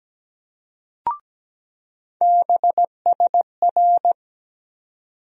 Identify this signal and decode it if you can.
Morse code, heard as a keyed tone, about 17 wpm, 700 Hz. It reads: BSR
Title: USA Call Sign Postfixes - 3 Letters 17wpm